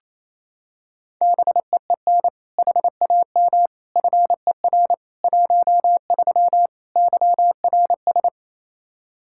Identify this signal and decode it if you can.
Morse code, heard as a keyed tone, about 28 wpm, 700 Hz. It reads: BEEN HAM FER 13 YRS
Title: Practice QSO - Exchange 3 - More Info - 28wpm